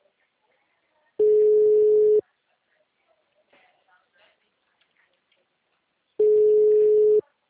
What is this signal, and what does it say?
Ringback tone (Europe, 425 Hz), with speech.